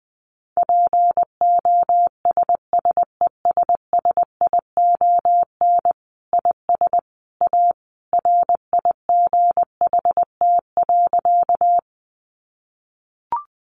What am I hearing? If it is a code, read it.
Morse code, heard as a keyed tone, about 20 wpm, 700 Hz. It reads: POSSESSION IS A RIGHT.
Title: Sentences from Top 1400 Words 20wpm